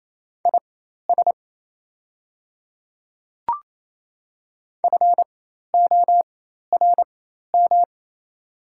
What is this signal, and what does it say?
Morse code, heard as a keyed tone, about 27 wpm, spaced as for 13 wpm, 690 Hz. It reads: IS FORM
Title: 4x Character Spacing - Top 200 Words - 28wpm